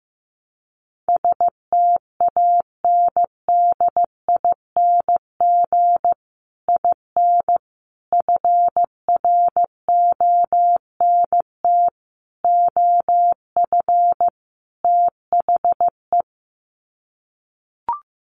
Morse code, heard as a keyed tone, about 15 wpm, 700 Hz. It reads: STANDING IN FRONT OF THE
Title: Sets of 5 Words Encore2 15wpm